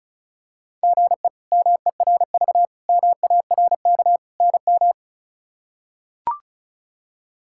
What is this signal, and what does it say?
Morse code, heard as a keyed tone, about 36 wpm, 700 Hz. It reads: GE MERV MARK NM
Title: K1USN SST QSO Practice - 35wpm